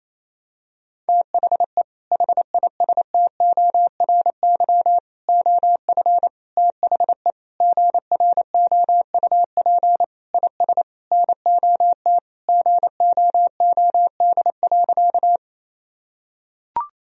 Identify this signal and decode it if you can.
Morse code, heard as a keyed tone, about 28 wpm, 700 Hz. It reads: THE HISTORY OF THE GROUP IS NOT GOOD.
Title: Sentences from Top 1000 Words - Review Part 1 - 28wpm